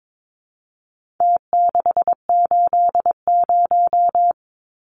Morse code, heard as a keyed tone, about 22 wpm, 700 Hz. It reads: T680
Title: Numbers - Contest Style - 4-Digit 22wpm